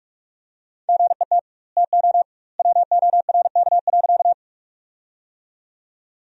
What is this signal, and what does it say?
Morse code, heard as a keyed tone, about 46 wpm, 700 Hz. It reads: GET TO WORK.
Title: 2x Word-Spacing - Sentences from Top 200 Words - 45wpm